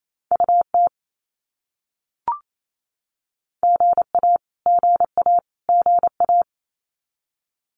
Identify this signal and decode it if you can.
Morse code, heard as a keyed tone, about 28 wpm, 700 Hz. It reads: UT GA GA GA
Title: US States Repeated 3x 28wpm